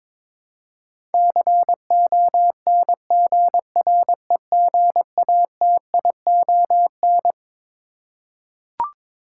Morse code, heard as a keyed tone, about 22 wpm, 700 Hz. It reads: CONGREGATION
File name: Twelve Letter Words - 22wpm